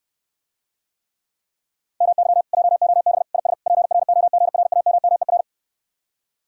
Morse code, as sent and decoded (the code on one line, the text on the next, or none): -- -.-- / .--- --- -... / .. ... / .--. .-. --- --. .-. .- -- -- . .-.
MY JOB IS PROGRAMMER